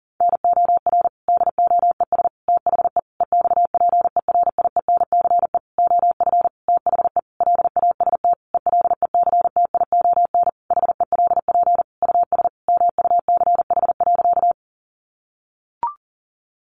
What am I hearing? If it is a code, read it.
Morse code, heard as a keyed tone, about 40 wpm, 700 Hz. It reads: NOR DOES THE EXPERIENCE OF THE LAST ELECTION HELP US MUCH.
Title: Sentences from Top 1500 Words 40wpm